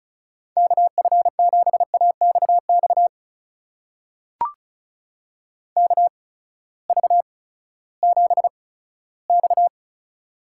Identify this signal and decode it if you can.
Morse code, heard as a keyed tone, about 34 wpm, spaced as for 17 wpm, 700 Hz. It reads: KF7AXX KV7X
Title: USA Call Signs - 8x Character Spacing - 35wpm